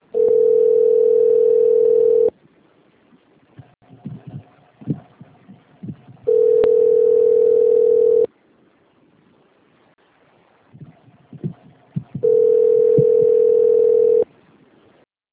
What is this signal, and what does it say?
Ringback tone (North America), 440 + 480 Hz.